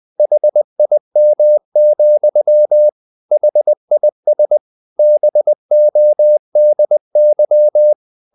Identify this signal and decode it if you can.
Morse code, heard as a keyed tone, about 20 wpm, 600 Hz. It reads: HIM, HIS BODY
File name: the old man and the sea / the old man and the sea part 1 20wpm